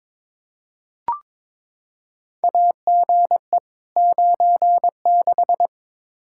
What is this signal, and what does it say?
Morse code, heard as a keyed tone, about 22 wpm, 700 Hz. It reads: AGE 96
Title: Practice QSO - Exchange 3 - More Info - 22wpm